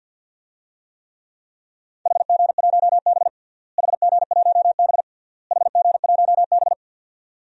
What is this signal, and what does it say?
Morse code, heard as a keyed tone, about 48 wpm, 700 Hz. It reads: HG1B HG1B HG1B
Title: DXCC-100 Call Signs - 4 Characters - Repeated 3x - 50wpm